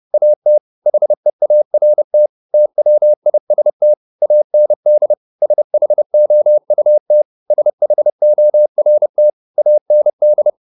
Morse code, heard as a keyed tone, about 30 wpm, 600 Hz. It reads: AT HEART TWIST AND SHOUT SHORT AND